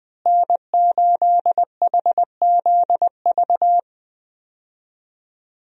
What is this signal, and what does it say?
Morse code, heard as a keyed tone, about 20 wpm, 700 Hz. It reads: N8HZV
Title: USA Call Signs - 5 Characters - Repeated 3x - 20wpm